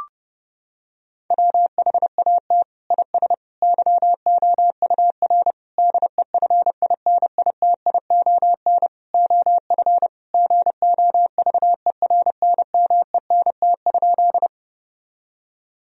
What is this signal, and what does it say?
Morse code, heard as a keyed tone, about 30 wpm, 700 Hz. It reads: WHAT IS YOUR DEFINITION OF GOVERNMENT?
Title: Sentences from Top 1000 Words - Review Part 4 - 30wpm